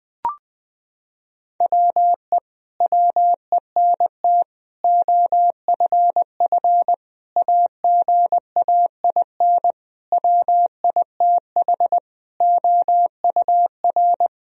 Morse code, heard as a keyed tone, about 20 wpm, 700 Hz. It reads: WE WENT OFF AGAIN WITH OUR
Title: Sentences from Top 1000 Words - Review Part 3 - 20wpm